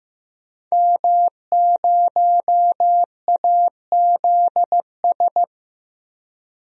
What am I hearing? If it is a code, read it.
Morse code, heard as a keyed tone, about 15 wpm, 700 Hz. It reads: M0AZS
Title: DXCC-100 Call Signs - 5 Characters - 15wpm